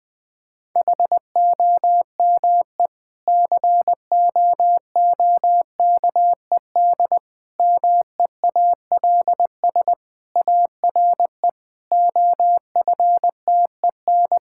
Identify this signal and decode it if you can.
Morse code, heard as a keyed tone, about 19 wpm, 700 Hz. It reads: HOME COOKED MEALS ARE OFTEN
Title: Difficult Sentences with Hints - 20wpm